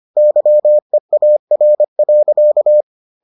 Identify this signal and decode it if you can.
Morse code, heard as a keyed tone, about 25 wpm, 600 Hz. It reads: YEAR.